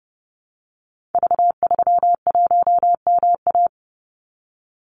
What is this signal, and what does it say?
Morse code, heard as a keyed tone, about 30 wpm, 700 Hz. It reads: V31MA
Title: Call Signs - 30wpm